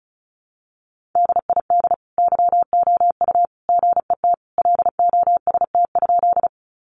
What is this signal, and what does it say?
Morse code, heard as a keyed tone, about 34 wpm, 700 Hz. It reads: DID YOU GET LOST?
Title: Sentences from Top 1400 Words 35wpm